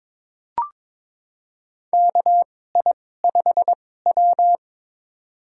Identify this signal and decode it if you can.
Morse code, heard as a keyed tone, about 22 wpm, spaced as for 16 wpm, 700 Hz. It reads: KI5W
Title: USA Call Signs - 4 Characters - 2x Character Spacing - 22wpm